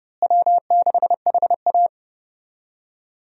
Morse code, heard as a keyed tone, about 30 wpm, 700 Hz. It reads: W6HA
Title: USA Call Signs - 4 Characters - Repeated 3x - 30wpm